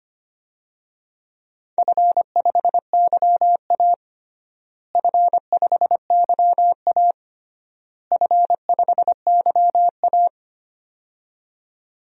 Morse code, heard as a keyed tone, about 25 wpm, 700 Hz. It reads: F5YA F5YA F5YA
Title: DXCC-100 Call Signs - 4 Characters - Repeated 3x - 25wpm